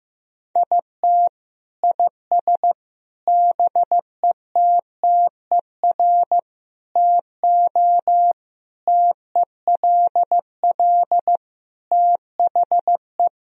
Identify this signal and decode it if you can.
Morse code, heard as a keyed tone, about 15 wpm, 700 Hz. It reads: IT IS BETTER TO TELL THE
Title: Sentences from Top 1000 Words - Review Part 4 - 15wpm